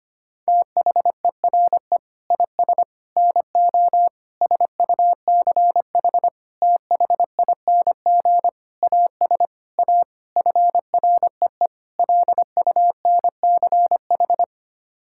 Morse code, heard as a keyed tone, about 25 wpm, 700 Hz. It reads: THERE IS NO SUCH THING AS A FREE LUNCH
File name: Sets of 9 Words - 25wpm